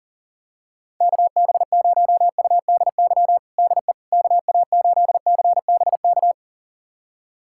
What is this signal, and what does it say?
Morse code, heard as a keyed tone, about 40 wpm, 700 Hz. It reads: KB0UDY DE KA8CBK
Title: USA Callsign DE Callsign - 6 Characters - 40wpm